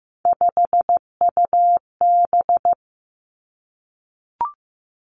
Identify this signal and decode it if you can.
Morse code, heard as a keyed tone, about 15 wpm, 700 Hz. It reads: 5UB
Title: Three Challenging Letter-Number Combinations 15wpm